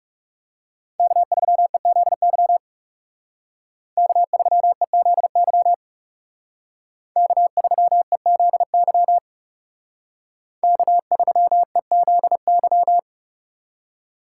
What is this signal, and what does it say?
Morse code, heard as a keyed tone, about 38 wpm, 700 Hz. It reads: K3EZY K3EZY K3EZY K3EZY